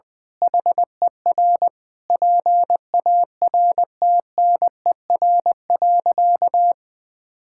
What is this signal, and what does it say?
Morse code, heard as a keyed tone, about 20 wpm, 700 Hz. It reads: HER PARTNER.
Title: Sentences with 8-Word Phrases - 20wpm